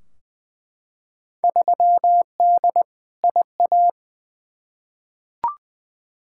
Morse code, heard as a keyed tone, about 20 wpm, 700 Hz. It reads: 3D IA
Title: Field Day Exchanges Encore - 20wpm